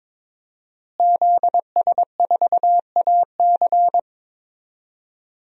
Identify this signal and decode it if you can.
Morse code, heard as a keyed tone, about 23 wpm, 700 Hz. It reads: ZS4AC
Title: DXCC-100 Call Signs - 5 Characters - Repeated 2x - 22wpm